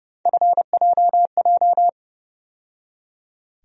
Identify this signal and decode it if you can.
Morse code, heard as a keyed tone, about 30 wpm, 700 Hz. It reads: FJJ
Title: USA Call Sign Postfixes - 3 Letters 30wpm